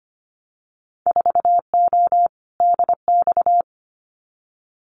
Morse code, heard as a keyed tone, about 25 wpm, 700 Hz. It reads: 4O DX
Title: Winter Field Day Exchanges Encore 25wpm